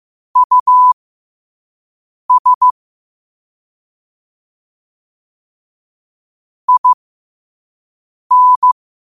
Morse code, heard as a keyed tone, about 15 wpm, spaced as for 5 wpm, 1000 Hz. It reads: US IN
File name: qotdfiles / 5wpm